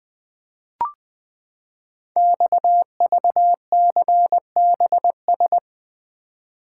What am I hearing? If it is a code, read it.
Morse code, heard as a keyed tone, about 20 wpm, 700 Hz. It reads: XVCBS
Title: Code Groups - 5 Letters - 20wpm